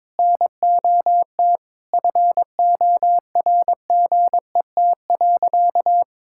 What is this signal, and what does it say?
Morse code, heard as a keyed tone, about 22 wpm, 700 Hz. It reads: NOT FORGET.
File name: practice / 22wpm / Sentences with 4-Word Phrases - 22wpm